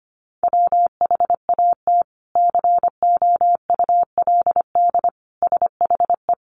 Morse code, heard as a keyed tone, about 25 wpm, 700 Hz. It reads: WHAT COULD SHE
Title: Sentences from Top 200 Words 25wpm